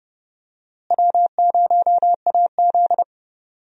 Morse code, heard as a keyed tone, about 31 wpm, 700 Hz. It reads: W0AZ
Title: USA Call Signs - 4 Characters - Repeated 3x - 30wpm